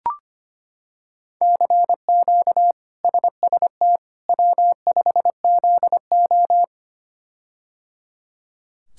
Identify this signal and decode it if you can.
Morse code, heard as a keyed tone, about 25 wpm, 700 Hz. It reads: CQ SST W5ZO
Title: K1USN SST QSO Practice - 25wpm